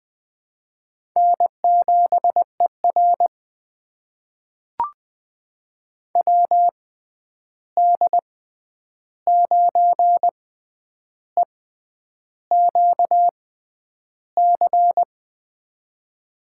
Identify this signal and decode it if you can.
Morse code, heard as a keyed tone, about 21 wpm, spaced as for 9 wpm, 700 Hz. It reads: N7ER WD9EQC
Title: USA Call Signs - 6x Character Spacing - 20wpm